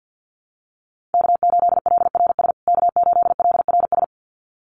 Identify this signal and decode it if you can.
Morse code, heard as a keyed tone, about 50 wpm, 700 Hz. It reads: K8LRS K8LRS
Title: USA Call Signs Repeated 2x 50wpm